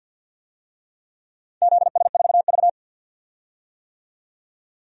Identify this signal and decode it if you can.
Morse code, heard as a keyed tone, about 52 wpm, 700 Hz. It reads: ZS4V
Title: DXCC-100 Call Signs - 4 Characters - 50wpm